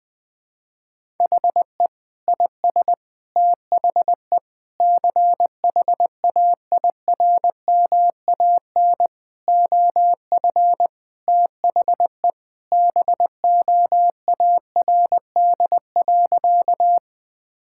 Morse code, heard as a keyed tone, about 21 wpm, 700 Hz. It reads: HE IS THE CHAIRMAN OF THE BOARD.
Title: Sentences with 3-Word Phrases - Spoken Beginning - 20wpm